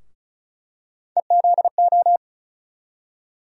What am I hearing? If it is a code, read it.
Morse code, heard as a keyed tone, about 36 wpm, 700 Hz. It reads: EZO
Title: USA Call Sign Postfixes - 3 Letters 35wpm